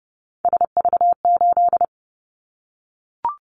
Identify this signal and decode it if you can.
Morse code, heard as a keyed tone, about 30 wpm, 700 Hz. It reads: SV8
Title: Callsign Prefixes for Top 50 DXCC - 30wpm